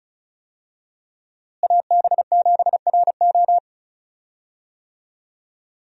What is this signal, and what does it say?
Morse code, heard as a keyed tone, about 34 wpm, 700 Hz. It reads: AB7RO
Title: USA Call Signs - 5 Characters - Repeated 3x - 35wpm